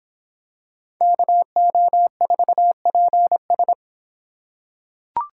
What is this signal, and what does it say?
Morse code, heard as a keyed tone, about 24 wpm, 700 Hz. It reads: KO4PS